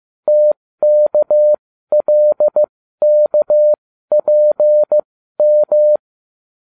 Morse code, heard as a keyed tone, about 15 wpm, spaced as for 12 wpm, 600 Hz. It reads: TKLKPM